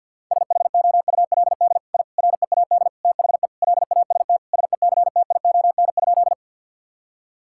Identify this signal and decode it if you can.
Morse code, heard as a keyed tone, about 50 wpm, 700 Hz. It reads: SHOULD I READ THE LAST SECTION?
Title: Sentences from Top 1000 Words - Review Part 5 - 50wpm